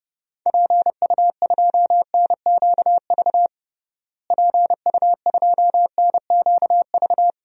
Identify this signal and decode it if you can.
Morse code, heard as a keyed tone, about 30 wpm, 700 Hz. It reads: PU2NQV PU2NQV
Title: DXCC-100 Call Signs - 6 Characters - Repeated 3x - 30wpm